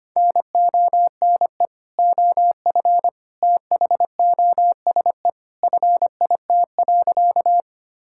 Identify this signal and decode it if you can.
Morse code, heard as a keyed tone, about 25 wpm, 700 Hz. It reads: NONE OF THOSE FIT.
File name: practice / 25wpm / Sentences with 2-Word Phrases - 25wpm